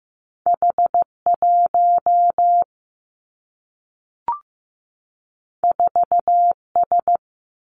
Morse code, heard as a keyed tone, about 15 wpm, 700 Hz. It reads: H 1 4 S
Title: Two Challenging Letter-Number Combinations 15wpm